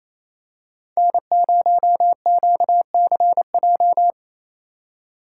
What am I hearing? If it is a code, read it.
Morse code, heard as a keyed tone, about 28 wpm, 700 Hz. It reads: N0QCJ
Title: USA Call Signs - 5 Characters - 2x Character Spacing - 28wpm